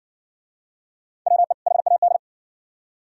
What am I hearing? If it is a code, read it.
Morse code, heard as a keyed tone, about 60 wpm, 700 Hz. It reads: WE HAD